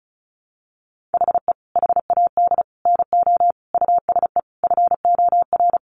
Morse code, heard as a keyed tone, about 34 wpm, 700 Hz. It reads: HE HAD NO USE FOR
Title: Sentences from Top 1200 Words 35wpm